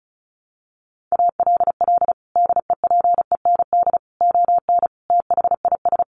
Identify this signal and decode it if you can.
Morse code, heard as a keyed tone, about 34 wpm, 700 Hz. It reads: ALL DEPEND ON THIS
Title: Sentences from Top 1300 Words 35wpm